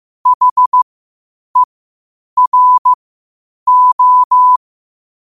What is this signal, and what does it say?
Morse code, heard as a keyed tone, about 15 wpm, spaced as for 8 wpm, 1000 Hz. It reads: HERO
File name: qotdfiles / 8wpm